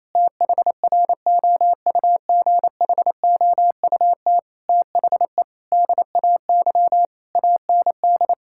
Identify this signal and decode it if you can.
Morse code, heard as a keyed tone, about 28 wpm, 700 Hz. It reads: THROUGHOUT THE DAY AND
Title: Sentences from Top 1000 Words - Review Part 6 - 28wpm